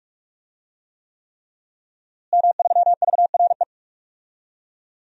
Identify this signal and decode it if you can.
Morse code, heard as a keyed tone, about 46 wpm, 700 Hz. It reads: M3VRE
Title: DXCC-100 Call Signs - 5 Characters - Repeated 3x - 45wpm